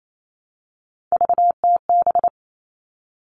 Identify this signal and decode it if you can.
Morse code, heard as a keyed tone, about 28 wpm, 700 Hz. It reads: VTB
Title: All Cut Numbers - 3-Digit 28wpm